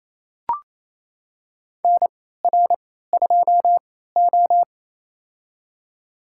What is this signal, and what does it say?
Morse code, heard as a keyed tone, about 28 wpm, spaced as for 16 wpm, 700 Hz. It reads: NR2O
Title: USA Call Signs - 4 Characters - 3x Character Spacing - 28wpm